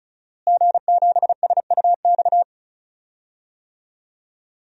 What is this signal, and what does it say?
Morse code, heard as a keyed tone, about 36 wpm, 700 Hz. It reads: G7SUX
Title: DXCC-100 Call Signs - 5 Characters - Repeated 2x - 35wpm